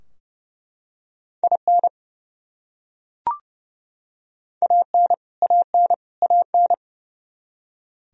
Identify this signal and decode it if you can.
Morse code, heard as a keyed tone, about 31 wpm, 700 Hz. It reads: IN AN AN AN
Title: Top 100 Words Repeated 30wpm